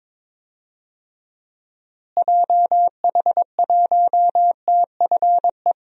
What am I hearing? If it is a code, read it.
Morse code, heard as a keyed tone, about 21 wpm, 700 Hz. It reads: JH1TFE
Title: DXCC-100 Call Signs - 6 Characters - Repeated 2x - 22wpm